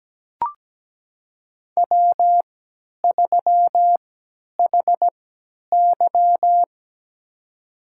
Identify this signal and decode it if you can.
Morse code, heard as a keyed tone, about 17 wpm, spaced as for 10 wpm, 700 Hz. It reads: W3HY
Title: USA Call Signs - 4 Characters - 3x Character Spacing - 17wpm